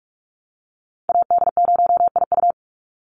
Morse code, heard as a keyed tone, about 44 wpm, 700 Hz. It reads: AD0IU
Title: Call Signs - 45wpm